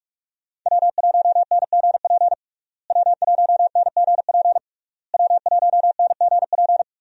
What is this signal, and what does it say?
Morse code, heard as a keyed tone, about 46 wpm, 700 Hz. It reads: W1NGP W1NGP W1NGP
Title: USA Call Signs - 5 Characters - Repeated 3x - 45wpm